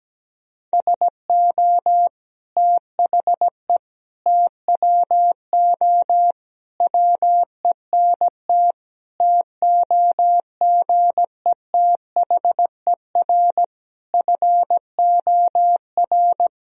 Morse code, heard as a keyed tone, about 17 wpm, 700 Hz. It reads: SO THE TWO WENT TOGETHER FOR
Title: Sentences from Top 1000 Words - Review Part 1 - 17wpm